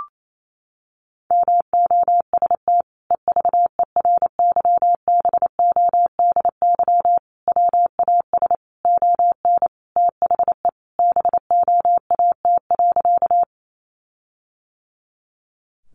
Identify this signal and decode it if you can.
Morse code, heard as a keyed tone, about 28 wpm, 700 Hz. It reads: MOST EVERYBODY WAS ON THE BOAT.
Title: Sentences from Top 1400 Words 28wpm